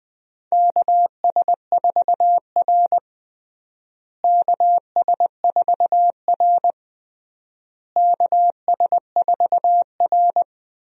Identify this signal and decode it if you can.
Morse code, heard as a keyed tone, about 20 wpm, 700 Hz. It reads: KS4R KS4R KS4R
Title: USA Call Signs - 4 Characters - Repeated 3x - 20wpm